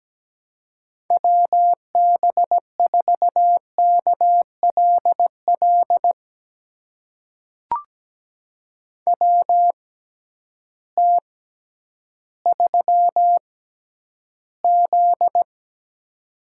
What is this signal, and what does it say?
Morse code, heard as a keyed tone, about 17 wpm, spaced as for 10 wpm, 700 Hz. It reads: WB4KLL WT3Z
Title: USA Call Signs - 6x Character Spacing - 17wpm